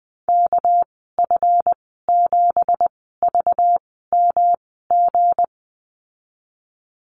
Morse code, heard as a keyed tone, about 20 wpm, spaced as for 15 wpm, 700 Hz. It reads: KF7VMG
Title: 213 Lesson 34g - Number 8 - Call Signs 20wpm